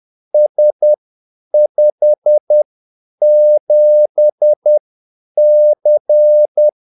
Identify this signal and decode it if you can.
Morse code, heard as a keyed tone, about 10 wpm, spaced as for 8 wpm, 600 Hz. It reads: S57C
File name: Callsigns-10-8wpm